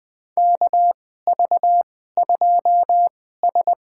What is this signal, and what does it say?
Morse code, heard as a keyed tone, about 21 wpm, spaced as for 15 wpm, 700 Hz. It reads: KV2S
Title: USA Call Signs - 2x Character Spacing - 20wpm